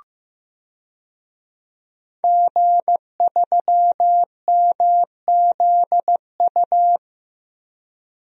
Morse code, heard as a keyed tone, about 15 wpm, 700 Hz. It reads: G3MZU
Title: DXCC-100 Call Signs - 5 Characters - 15wpm